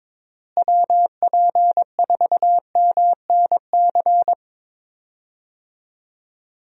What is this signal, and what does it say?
Morse code, heard as a keyed tone, about 22 wpm, 700 Hz. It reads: WP4MNC
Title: USA Call Signs - 6 Characters - Repeated 2x - 22wpm